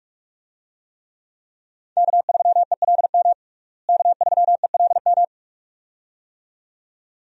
Morse code, heard as a keyed tone, about 46 wpm, 700 Hz. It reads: K3ELM K3ELM
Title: USA Call Signs - 5 Characters - Repeated 2x - 45wpm